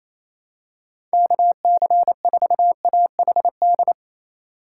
Morse code, heard as a keyed tone, about 28 wpm, 700 Hz. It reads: KC4AHD
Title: USA Call Signs - 6 Characters - 3x Character Spacing - 28wpm